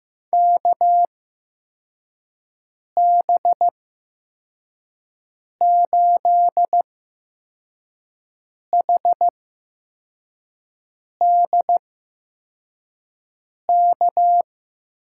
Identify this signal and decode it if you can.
Morse code, heard as a keyed tone, about 15 wpm, spaced as for 4 wpm, 700 Hz. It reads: KB8HDK
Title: USA Call Signs - 6 Characters - 8x Character Spacing - 15wpm